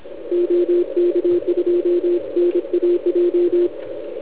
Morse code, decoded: OK2NAJ